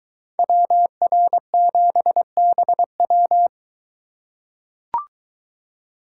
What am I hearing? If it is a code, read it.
Morse code, heard as a keyed tone, about 22 wpm, 700 Hz. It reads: WR7BW